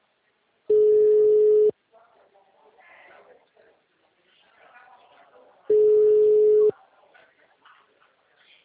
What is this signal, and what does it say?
Ringback tone (Europe, 425 Hz), with speech.